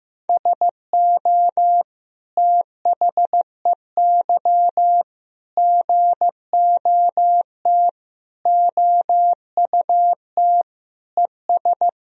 Morse code, heard as a keyed tone, about 15 wpm, 700 Hz. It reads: SO THEY GOT OUT ES
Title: Sentences from Top 300 Words and Abbreviations 15wpm